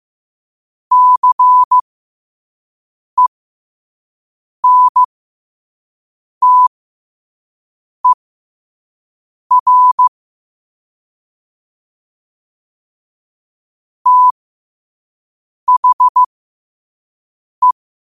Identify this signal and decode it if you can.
Morse code, heard as a keyed tone, about 15 wpm, spaced as for 5 wpm, 1000 Hz. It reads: CENTER THE
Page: qotd-at-5-wpm